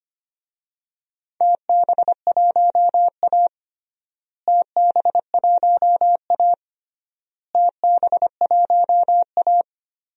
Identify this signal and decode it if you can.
Morse code, heard as a keyed tone, about 25 wpm, 700 Hz. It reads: TB1A TB1A TB1A
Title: DXCC-100 Call Signs - 4 Characters - Repeated 3x - 25wpm